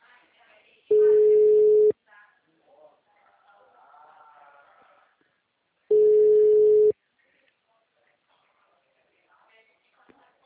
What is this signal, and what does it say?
Ringback tone (Europe, 425 Hz), with speech.
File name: out- -2002- - - .1868